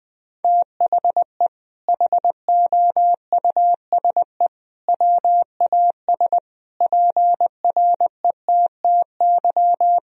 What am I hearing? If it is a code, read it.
Morse code, heard as a keyed tone, about 20 wpm, 700 Hz. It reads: THE HOUSE WAS PRETTY
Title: Sentences from Top 1000 Words - Review Part 3 - 20wpm